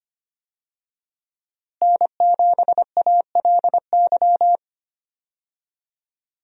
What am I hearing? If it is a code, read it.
Morse code, heard as a keyed tone, about 25 wpm, 700 Hz. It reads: N7ALY
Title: USA Call Signs - 5 Characters - Repeated 2x - 25wpm